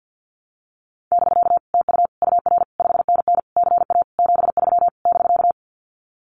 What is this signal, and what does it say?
Morse code, heard as a keyed tone, about 50 wpm, 700 Hz. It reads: <BK> TU UR 5NN CA 73 <BK>
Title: Common QSO Phrases Encore 50wpm